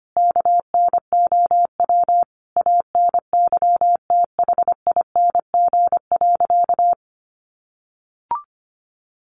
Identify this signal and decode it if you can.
Morse code, heard as a keyed tone, about 25 wpm, 700 Hz. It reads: KNOW ANYTHING.
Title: A Sound Of Thunder 25wpm